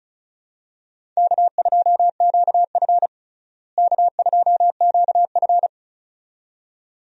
Morse code, heard as a keyed tone, about 34 wpm, 700 Hz. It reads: K2QF K2QF
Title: USA Call Signs - 4 Characters - Repeated 2x - 35wpm